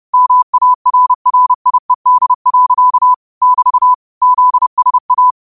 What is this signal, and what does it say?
Morse code, heard as a keyed tone, about 30 wpm, 1000 Hz. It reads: MARRIED. <BT> ZSA
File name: qotdfiles / 30wpm